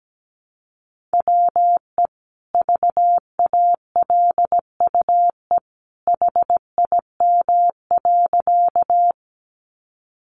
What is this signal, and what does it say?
Morse code, heard as a keyed tone, about 17 wpm, 700 Hz. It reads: WE VALUE HIM.
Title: Sentences from Top 600 Words 17wpm